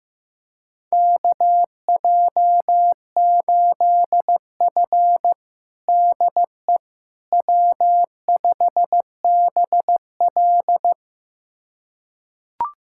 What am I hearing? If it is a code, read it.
Morse code, heard as a keyed tone, about 15 wpm, 700 Hz. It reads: KJ8F DE W5BL
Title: USA Callsign DE Callsign - 4 Characters - 15wpm